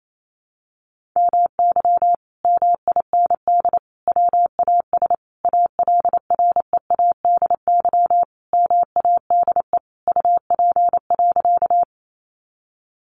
Morse code, heard as a keyed tone, about 28 wpm, 700 Hz. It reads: MY MIND WAS ALREADY MADE UP.
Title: Sentences from Top 700 Words 28wpm